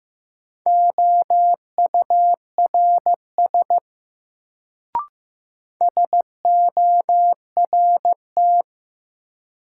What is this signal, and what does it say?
Morse code, heard as a keyed tone, about 15 wpm, 700 Hz. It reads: OURS SORT
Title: Sentences from Top 1300 Words - New Word Drill Alt - 15wpm